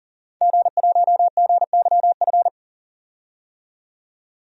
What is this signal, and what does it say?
Morse code, heard as a keyed tone, about 38 wpm, 700 Hz. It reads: G1GYF
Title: DXCC-100 Call Signs - 5 Characters - 40wpm